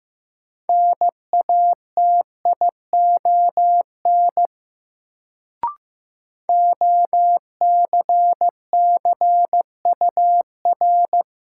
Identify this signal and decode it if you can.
Morse code, heard as a keyed tone, about 15 wpm, 700 Hz. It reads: NATION OCCUR
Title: Sentences from Top 900 Words - New Word Drill - 15wpm